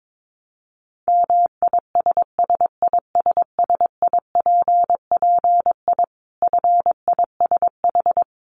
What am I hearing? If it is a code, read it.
Morse code, heard as a keyed tone, about 22 wpm, 700 Hz. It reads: MISSISSIPPI FISH
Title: Dit Challenge 22wpm